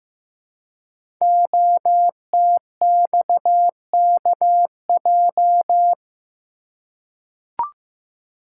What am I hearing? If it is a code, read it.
Morse code, heard as a keyed tone, about 15 wpm, 700 Hz. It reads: OTXKJ
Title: Code Groups - 5 Letter:Numbers - 15wpm